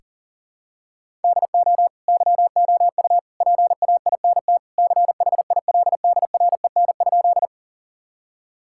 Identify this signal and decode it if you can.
Morse code, heard as a keyed tone, about 40 wpm, 700 Hz. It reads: DO YOU PAINT CHILDREN?
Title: Sentences from Top 1000 Words - Review Part 6 - 40wpm